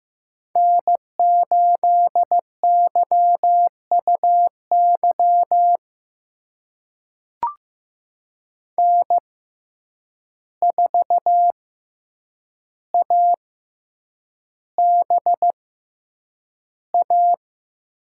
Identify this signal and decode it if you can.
Morse code, heard as a keyed tone, about 15 wpm, spaced as for 8 wpm, 700 Hz. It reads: N8YUY N4ABA
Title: USA Call Signs - 5 Characters - 6x Character Spacing - 15wpm